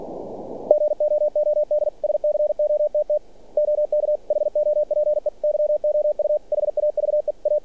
Morse code, heard as a keyed tone, about 46 wpm, 600 Hz. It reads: GOOD SCOTT OK HOPE YOU HAVE A
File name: archived / CRSnet